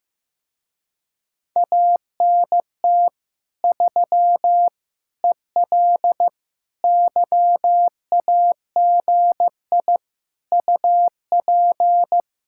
Morse code, heard as a keyed tone, about 15 wpm, 700 Hz. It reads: ANT 3 EL YAGI UP
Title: Speed-Racing Common QSO Phrases Encore - 15wpm